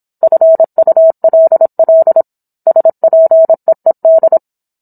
Morse code, heard as a keyed tone, about 27 wpm, 650 Hz. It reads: FULL SPEED
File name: Book One / 17-Chapter 17 - 26 WPM